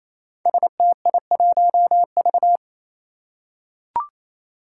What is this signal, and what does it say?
Morse code, heard as a keyed tone, about 28 wpm, 700 Hz. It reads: STI1V
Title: Five Challenging Letter-Number Combinations 28wpm